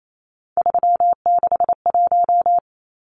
Morse code, heard as a keyed tone, about 28 wpm, 700 Hz. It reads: 361
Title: Numbers - 3-Digit 28wpm